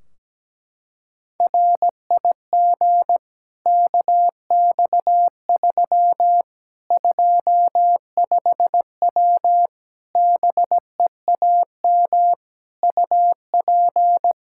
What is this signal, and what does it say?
Morse code, heard as a keyed tone, about 17 wpm, 700 Hz. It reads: RIG KX3 25W BEAM UP
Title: K1USN SST QSO Practice 17wpm